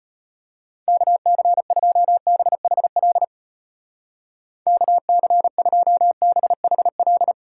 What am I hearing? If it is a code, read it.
Morse code, heard as a keyed tone, about 32 wpm, 700 Hz. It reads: KC2BHL KC2BHL